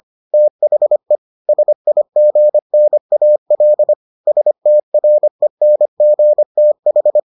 Morse code, heard as a keyed tone, about 25 wpm, 600 Hz. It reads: THE SIGNAL STRENGTH